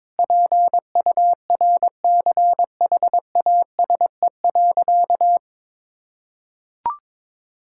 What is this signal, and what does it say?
Morse code, heard as a keyed tone, about 22 wpm, 700 Hz. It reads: PURCHASE.
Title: Sentences from Top 1000 Words - Review Part 5 - 22wpm